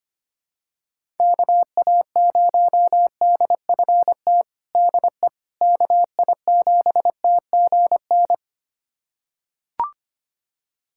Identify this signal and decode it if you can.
Morse code, heard as a keyed tone, about 25 wpm, 700 Hz. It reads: KA0DFT DE KI7TGN
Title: USA Callsign DE Callsign - 6 Characters - 25wpm